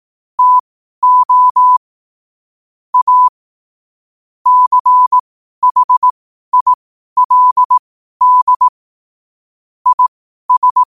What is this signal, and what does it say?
Morse code, heard as a keyed tone, about 18 wpm, spaced as for 12 wpm, 1000 Hz. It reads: TO A CHILD IS
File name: qotdfiles / 12wpm